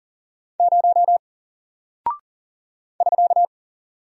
Morse code, heard as a keyed tone, about 40 wpm, 700 Hz. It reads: 0 <SK>
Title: Single Character (Letter:Number:Punctuation:Prosign) Alt 40wpm